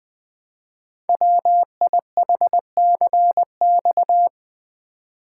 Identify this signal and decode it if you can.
Morse code, heard as a keyed tone, about 20 wpm, 700 Hz. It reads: WIHCX
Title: Code Groups - 5 Letters - 20wpm